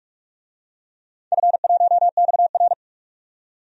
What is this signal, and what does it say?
Morse code, heard as a keyed tone, about 41 wpm, 700 Hz. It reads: F1XR
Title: DXCC-100 Call Signs - 4 Characters - 45wpm